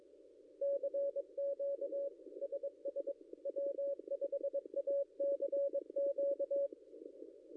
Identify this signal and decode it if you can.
Morse code, heard as a keyed tone, about 22 wpm, 560 Hz. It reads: CQ SS W5ACQ
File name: contests / ss-cw-k5tr-op-n5rz -0800